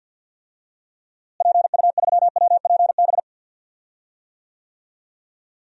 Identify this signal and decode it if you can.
Morse code, heard as a keyed tone, about 52 wpm, 700 Hz. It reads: PU3WPB
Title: DXCC-100 Call Signs - 6 Characters - Repeated 3x - 50wpm